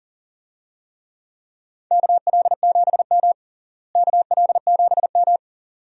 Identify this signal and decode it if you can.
Morse code, heard as a keyed tone, about 40 wpm, 700 Hz. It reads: KL7M KL7M
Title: DXCC-100 Call Signs - 4 Characters - Repeated 2x - 40wpm